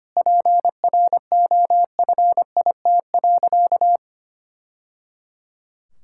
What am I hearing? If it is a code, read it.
Morse code, heard as a keyed tone, about 25 wpm, 700 Hz. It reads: PROFIT.
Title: Sentences from Top 1000 Words - Review Part 5 - 25wpm